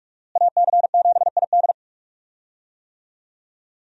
Morse code, heard as a keyed tone, about 46 wpm, 700 Hz. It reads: AC7ID
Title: USA Call Signs - 5 Characters - Repeated 2x - 45wpm